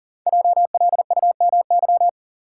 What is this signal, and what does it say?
Morse code, heard as a keyed tone, about 40 wpm, 700 Hz. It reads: JLUMY
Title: Code Groups - 5 Letters - 40wpm